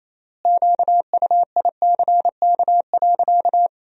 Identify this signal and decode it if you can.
Morse code, heard as a keyed tone, about 28 wpm, 700 Hz. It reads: QUICK.
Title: 3x Word-Spacing - Sentences from Top 500 Words - 28wpm